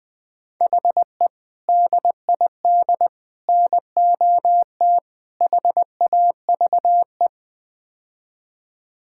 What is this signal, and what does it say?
Morse code, heard as a keyed tone, about 21 wpm, 700 Hz. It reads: HE DID NOT HAVE
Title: Build A Phrase - 5-Word Phrases - 20wpm